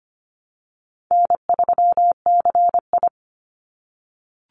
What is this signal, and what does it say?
Morse code, heard as a keyed tone, about 25 wpm, 700 Hz. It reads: N3CI
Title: Call Signs - 25wpm